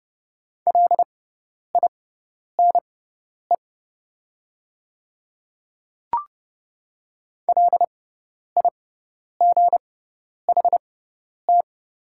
Morse code, heard as a keyed tone, about 30 wpm, spaced as for 10 wpm, 700 Hz. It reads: LINE LIGHT
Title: 6x Character Spacing - Top 500 Words - 30wpm